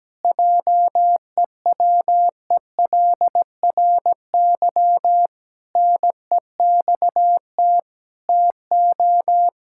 Morse code, heard as a keyed tone, about 17 wpm, 700 Hz. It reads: JEWELRY NEXT TO
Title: Pangram Challenge v2 - 17wpm